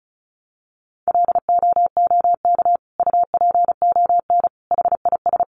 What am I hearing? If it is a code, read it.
Morse code, heard as a keyed tone, about 34 wpm, 700 Hz. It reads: LOOK UPON HIS
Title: Sentences from Top 1100 Words 35wpm